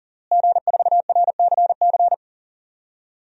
Morse code, heard as a keyed tone, about 40 wpm, 700 Hz. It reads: G4RCC
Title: DXCC-100 Call Signs - 5 Characters - Repeated 2x - 40wpm